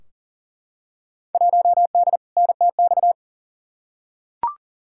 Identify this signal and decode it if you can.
Morse code, heard as a keyed tone, about 40 wpm, 700 Hz. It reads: 1D NTX
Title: Field Day Exchanges - 40wpm